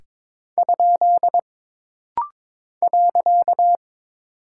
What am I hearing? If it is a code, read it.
Morse code, heard as a keyed tone, about 22 wpm, 700 Hz. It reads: ? .
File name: practice / 22wpm / Single Punctuation-Prosign - Rapid-Fire - 22wpm